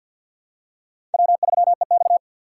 Morse code, heard as a keyed tone, about 49 wpm, 700 Hz. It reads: W3EX
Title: USA Call Signs - 4x Character Spacing - 50wpm